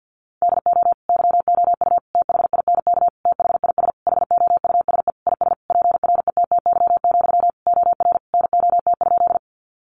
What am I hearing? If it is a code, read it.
Morse code, heard as a keyed tone, about 50 wpm, 700 Hz. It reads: DO YOU THINK THIS HOUSE IS PRETTY, OR NOT?
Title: Sentences from Top 1000 Words 50wpm